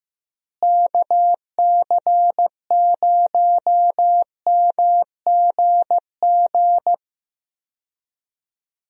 Morse code, heard as a keyed tone, about 15 wpm, 700 Hz. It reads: KC0MGG
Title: USA Call Signs - 6 Characters - Repeated 2x - 15wpm